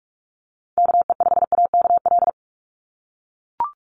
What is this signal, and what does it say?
Morse code, heard as a keyed tone, about 44 wpm, 700 Hz. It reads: KE5AKL
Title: NA SOTA Chaser Callsigns 45wpm